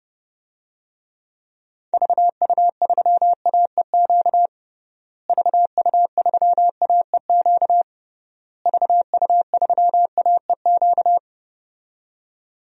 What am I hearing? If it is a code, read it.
Morse code, heard as a keyed tone, about 32 wpm, 700 Hz. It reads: VU3AEQ VU3AEQ VU3AEQ
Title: DXCC-100 Call Signs - 6 Characters - Repeated 3x - 30wpm